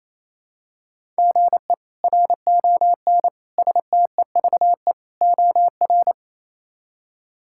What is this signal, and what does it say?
Morse code, heard as a keyed tone, about 28 wpm, 700 Hz. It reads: GE RON STEVE OR
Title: K1USN SST QSO Practice - 28wpm